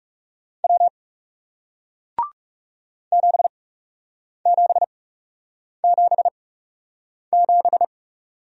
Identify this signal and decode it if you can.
Morse code, heard as a keyed tone, about 40 wpm, spaced as for 8 wpm, 700 Hz. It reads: W 7777